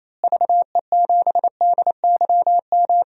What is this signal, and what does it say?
Morse code, heard as a keyed tone, about 27 wpm, 700 Hz. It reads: VE7DYM
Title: DXCC-100 Call Signs - 6 Characters - Repeated 2x - 28wpm